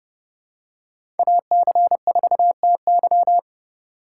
Morse code, heard as a keyed tone, about 30 wpm, 700 Hz. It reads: AC4TY
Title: USA Call Signs - 5 Characters - 2x Character Spacing - 30wpm